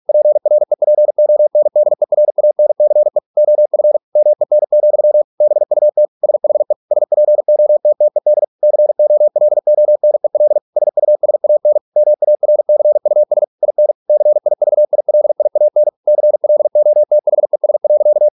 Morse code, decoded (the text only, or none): PREPONDERANCE OF MEN, BUT SHE SPOTTED COLONEL SUSAN MARCUS IN CIVILIAN CLOTHES.